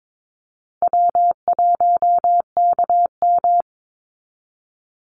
Morse code, heard as a keyed tone, about 22 wpm, 690 Hz. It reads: W1KM
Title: Easy Call Signs Repeated 2x - 22wpm